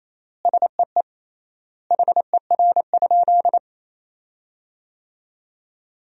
Morse code, heard as a keyed tone, about 28 wpm, 700 Hz. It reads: SEE HER?
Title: 3x Word-Spacing - Sentences from Top 100 Words - 28wpm